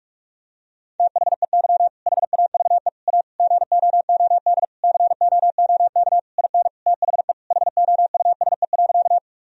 Morse code, heard as a keyed tone, about 41 wpm, 700 Hz. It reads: THEY HAVE A GOOD COOK IN THE HOUSE.
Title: Sentences from Top 1000 Words - Review Part 4 - 45wpm